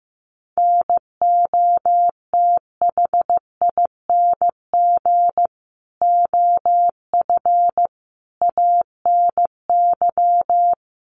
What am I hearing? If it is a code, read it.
Morse code, heard as a keyed tone, about 14 wpm, 700 Hz. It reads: NOTHING OF ANY
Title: Sentences from Top 1200 Words 15wpm